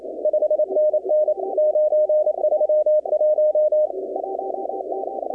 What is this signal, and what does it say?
Morse code, heard as a keyed tone, about 27 wpm, 610 Hz. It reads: 5NN 931
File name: contests / 2019-cw-wpx-kt5j-k5pi- -2330 0-20